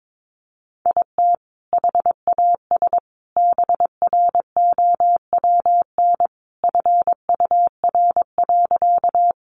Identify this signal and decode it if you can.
Morse code, heard as a keyed tone, about 22 wpm, 700 Hz. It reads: IT HAS BROWN FUR.
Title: Easy Phrases 22wpm